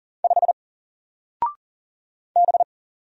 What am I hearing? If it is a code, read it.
Morse code, heard as a keyed tone, about 40 wpm, 700 Hz. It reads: 5 B
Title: Single Letter-Number 40wpm